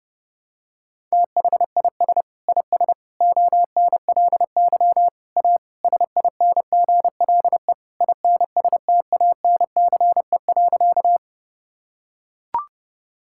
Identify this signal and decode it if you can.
Morse code, heard as a keyed tone, about 32 wpm, 700 Hz. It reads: THIS IS ONLY A SINGLE INSTANCE.
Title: Sentences from Top 1000 Words - Review Part 5 - 30wpm